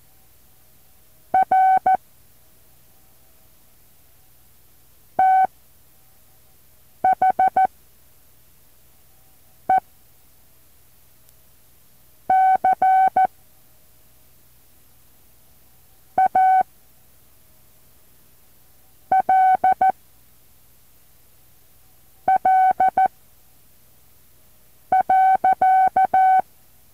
Morse code, heard as a keyed tone, about 14 wpm, spaced as for 4 wpm, 760 Hz. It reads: R THE CALL.
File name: intl morse code2